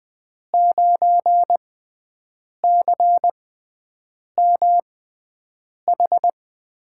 Morse code, heard as a keyed tone, about 20 wpm, spaced as for 7 wpm, 700 Hz. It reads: 9CMH
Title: USA Call Signs - 5 Characters - 6x Character Spacing - 20wpm